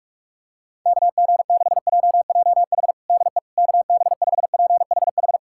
Morse code, heard as a keyed tone, about 46 wpm, 700 Hz. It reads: KG6JJH DE KB5PHH
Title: USA Callsign DE Callsign 45wpm